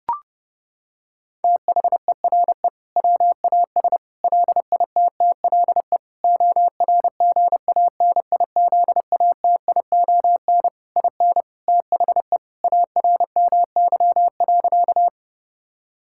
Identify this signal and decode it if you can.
Morse code, heard as a keyed tone, about 30 wpm, 700 Hz. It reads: THERE WAS LITTLE ORGANIZATION IN THE ARMY.
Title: Sentences from Top 1000 Words - Review Part 5 - 30wpm